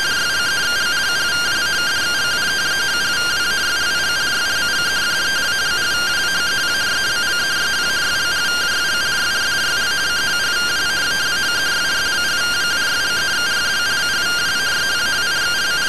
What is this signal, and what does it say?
Two-tone teleprinter signal, about 170 Hz shift.